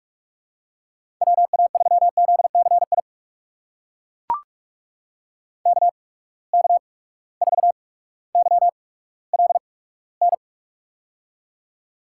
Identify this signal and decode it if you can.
Morse code, heard as a keyed tone, about 42 wpm, spaced as for 19 wpm, 700 Hz. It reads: WA3ZCI KK4YLN
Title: USA Call Signs - 6 Characters - 8x Character Spacing - 45wpm